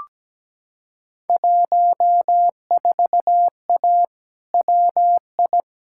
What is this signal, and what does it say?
Morse code, heard as a keyed tone, about 17 wpm, 700 Hz. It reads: 14A WI
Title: Field Day Exchanges Encore - 17wpm